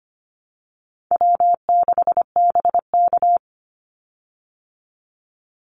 Morse code, heard as a keyed tone, about 25 wpm, 700 Hz. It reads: W6BK
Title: Easy Call Signs Repeated 3x - 25wpm